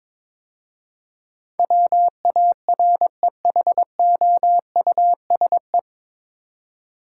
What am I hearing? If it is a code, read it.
Morse code, heard as a keyed tone, about 22 wpm, 700 Hz. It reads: WAREHOUSE
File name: 9 Letter Words - First 3 Letters Given - 22wpm